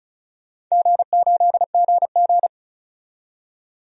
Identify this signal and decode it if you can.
Morse code, heard as a keyed tone, about 36 wpm, 700 Hz. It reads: G8GG
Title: DXCC-100 Call Signs - 4 Characters - 35wpm